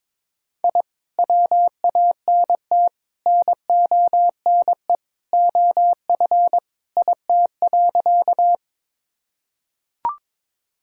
Morse code, heard as a keyed tone, about 21 wpm, 700 Hz. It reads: I WANT NONE OF IT.
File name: practice / 22wpm / Sentences with 2-Word Phrases - 22wpm